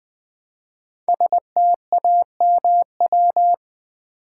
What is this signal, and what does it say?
Morse code, heard as a keyed tone, about 20 wpm, 700 Hz. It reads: STAMW
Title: Code Groups - 5 Letters - 20wpm